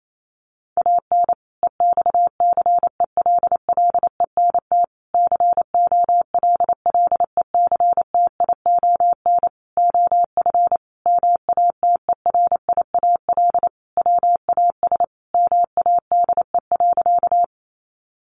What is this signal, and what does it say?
Morse code, heard as a keyed tone, about 28 wpm, 700 Hz. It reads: AN EXCELLENT COLLECTION OF MATERIAL WAS MADE.
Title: Sentences from Top 1200 Words 28wpm